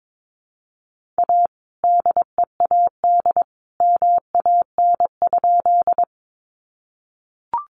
Morse code, heard as a keyed tone, about 22 wpm, 700 Hz. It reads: A DEAD MAN?
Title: Sentences from Top 1500 Words 22wpm